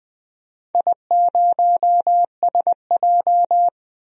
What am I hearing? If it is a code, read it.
Morse code, heard as a keyed tone, about 20 wpm, 700 Hz. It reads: I0SJ
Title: DXCC-100 Call Signs - 4 Characters - 20wpm